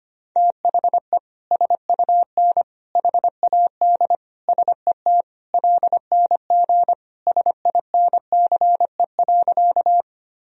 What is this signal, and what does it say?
Morse code, heard as a keyed tone, about 25 wpm, 700 Hz. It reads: THE SUN HAD SET LNG SINCE.
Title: Sentences from Top 300 Words and Abbreviations 25wpm